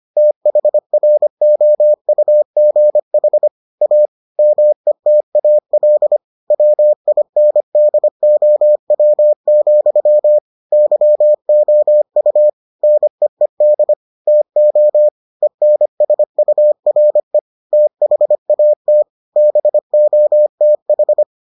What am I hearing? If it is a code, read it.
Morse code, heard as a keyed tone, about 25 wpm, 600 Hz. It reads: THROUGH A METAL WINDOW, YOU NEED TO ENSURE THAT BOTH